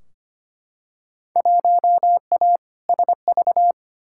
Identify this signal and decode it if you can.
Morse code, heard as a keyed tone, about 25 wpm, 700 Hz. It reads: 1A SV
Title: Field Day Exchanges - 25wpm